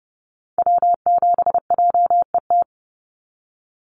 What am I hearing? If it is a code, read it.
Morse code, heard as a keyed tone, about 30 wpm, 700 Hz. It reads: W7JET
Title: NA SOTA Activator Callsigns 30wpm